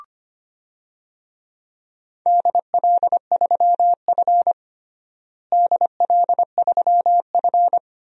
Morse code, heard as a keyed tone, about 25 wpm, 700 Hz. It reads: DL3F DL3F
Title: DXCC-100 Call Signs - 4 Characters - Repeated 2x - 25wpm